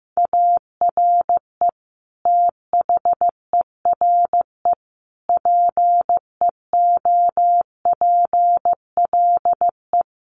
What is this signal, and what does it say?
Morse code, heard as a keyed tone, about 15 wpm, 700 Hz. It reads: ARE THERE PEOPLE
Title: Sentences from Top 1400 Words 15wpm